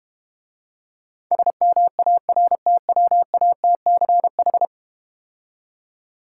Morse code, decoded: SMARTWATCH

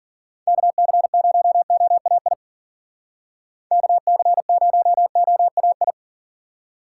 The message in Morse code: KC0OAI KC0OAI

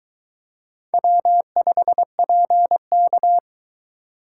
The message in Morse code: W5PK

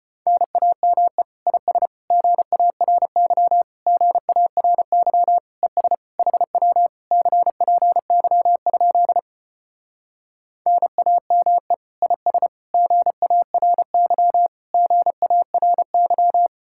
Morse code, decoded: NAME IS GARY GARY ES HW CPY? NAME IS GARY GARY

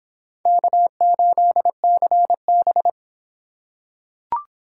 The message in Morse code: K8CB

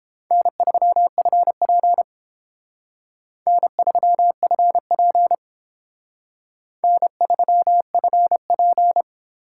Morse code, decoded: N3FP N3FP N3FP

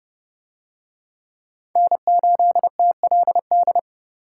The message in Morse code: N8TLD

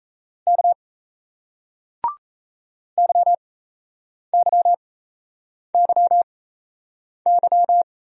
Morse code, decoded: K YYYY

